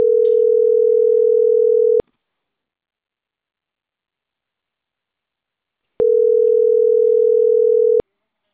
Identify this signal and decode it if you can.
Ringback tone (North America), 440 + 480 Hz.